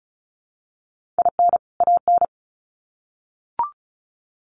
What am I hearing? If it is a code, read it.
Morse code, heard as a keyed tone, about 34 wpm, 700 Hz. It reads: IN AN
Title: Sets of 2 Words 35wpm